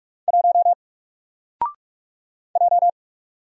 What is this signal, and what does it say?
Morse code, heard as a keyed tone, about 44 wpm, 700 Hz. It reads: 1 J
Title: Single Character (Letter:Number:Punctuation:Prosign) 45wpm